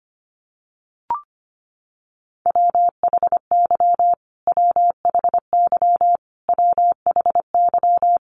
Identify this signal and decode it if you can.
Morse code, heard as a keyed tone, about 25 wpm, 700 Hz. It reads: WHY WHY WHY
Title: Top 500 Words Repeated 25wpm